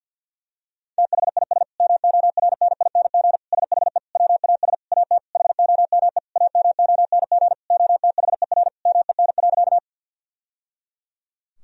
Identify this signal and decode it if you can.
Morse code, heard as a keyed tone, about 50 wpm, 700 Hz. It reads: THIS MORNING SHE WAS AT HOME AMONG OTHER MEN.
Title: Sentences from Top 1000 Words - Review Part 3 - 50wpm